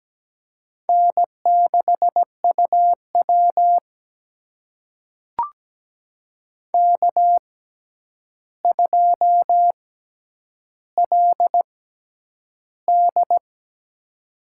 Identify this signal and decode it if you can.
Morse code, heard as a keyed tone, about 17 wpm, spaced as for 9 wpm, 700 Hz. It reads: N6UW K2LD
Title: USA Call Signs - 4 Characters - 6x Character Spacing - 17wpm